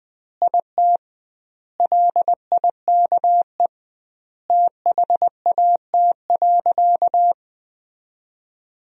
Morse code, heard as a keyed tone, about 21 wpm, 700 Hz. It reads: IT LIKE THAT.
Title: 2x Word-Spacing - Sentences from Top 300 Words - 20wpm